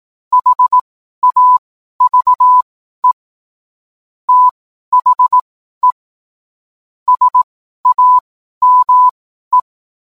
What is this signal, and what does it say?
Morse code, heard as a keyed tone, about 18 wpm, spaced as for 12 wpm, 1000 Hz. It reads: HAVE THE SAME